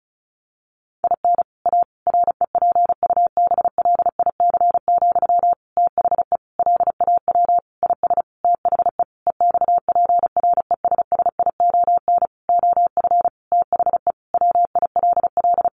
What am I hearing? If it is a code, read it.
Morse code, heard as a keyed tone, about 34 wpm, 700 Hz. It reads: IN A REPUBLIC, THE LAW IS THE EXPRESSION OF THE WILL
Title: Sentences from Top 1500 Words 35wpm